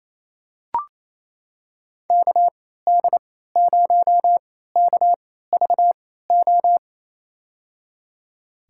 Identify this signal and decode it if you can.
Morse code, heard as a keyed tone, about 28 wpm, spaced as for 16 wpm, 700 Hz. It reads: KD0KVO
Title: USA Call Signs - 6 Characters - 3x Character Spacing - 28wpm